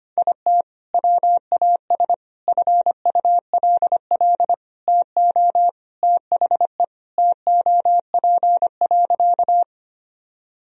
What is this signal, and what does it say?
Morse code, heard as a keyed tone, about 25 wpm, 700 Hz. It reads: IT WAS FULL TO THE TOP.
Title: Sentences from Top 1000 Words - Review Part 3 - 25wpm